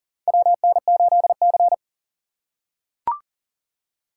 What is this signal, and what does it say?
Morse code, heard as a keyed tone, about 40 wpm, 700 Hz. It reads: WN8C
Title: USA Call Signs - 4 Characters - 4x Character Spacing - 40wpm